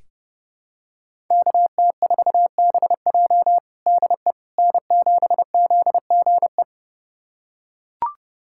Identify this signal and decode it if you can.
Morse code, heard as a keyed tone, about 30 wpm, 700 Hz. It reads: KT4BJ DE N7ZGE
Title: USA Callsign DE Callsign - 5 Characters - 30wpm